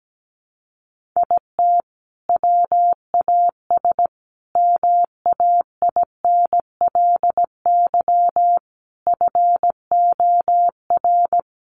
Morse code, heard as a keyed tone, about 17 wpm, 700 Hz. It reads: IT WAS MAINLY FOR
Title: Sentences from Top 1400 Words 17wpm